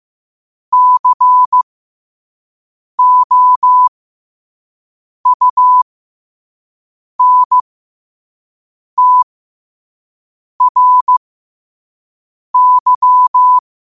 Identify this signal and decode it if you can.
Morse code, heard as a keyed tone, about 15 wpm, spaced as for 5 wpm, 1000 Hz. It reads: COUNTRY
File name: qotdfiles / 5wpm